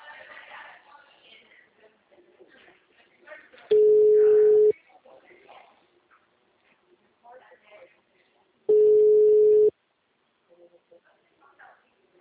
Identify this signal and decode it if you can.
Ringback tone (Europe, 425 Hz), with speech.
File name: out- -2014- - - .34783